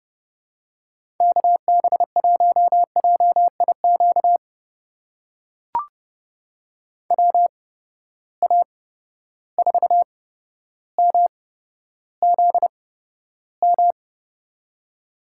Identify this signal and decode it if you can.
Morse code, heard as a keyed tone, about 29 wpm, spaced as for 13 wpm, 700 Hz. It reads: KB1JIQ WA4MZM
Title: USA Call Signs - 6 Characters - 8x Character Spacing - 30wpm